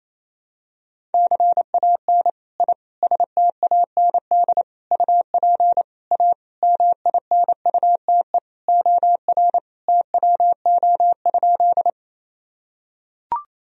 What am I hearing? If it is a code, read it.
Morse code, heard as a keyed tone, about 28 wpm, 700 Hz. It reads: CAN I STAND UP A MINUTE OR TWO?
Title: Sentences from Top 1000 Words - Review Part 1 - 28wpm